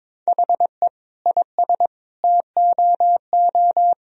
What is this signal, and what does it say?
Morse code, heard as a keyed tone, about 21 wpm, 700 Hz. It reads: HE IS TOO
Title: Sentences from Top 1000 Words - Review Part 4 - 22wpm